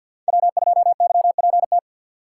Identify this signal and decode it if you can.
Morse code, heard as a keyed tone, about 53 wpm, 700 Hz. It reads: W2YPT
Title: USA Call Signs - 5 Characters - 4x Character Spacing - 50wpm